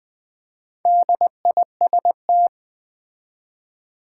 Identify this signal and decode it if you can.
Morse code, heard as a keyed tone, about 20 wpm, 700 Hz. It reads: DIST
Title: Build A Word - 4 Letters At A Time - 11-Letter Words - 20wpm